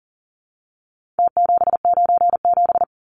Morse code, heard as a keyed tone, about 40 wpm, 700 Hz. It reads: T797
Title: Numbers - Contest Style - 4-Digit 40wpm